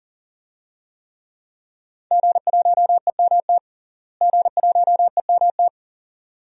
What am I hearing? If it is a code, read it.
Morse code, heard as a keyed tone, about 40 wpm, 700 Hz. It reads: G1EMT G1EMT
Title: DXCC-100 Call Signs - 5 Characters - Repeated 2x - 40wpm